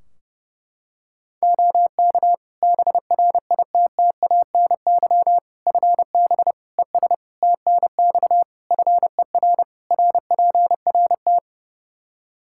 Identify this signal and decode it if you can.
Morse code, heard as a keyed tone, about 30 wpm, 700 Hz. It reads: OK BRITTANY FB ES TNX FER RPRT
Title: Practice QSO - Exchange 2 - Info - 30wpm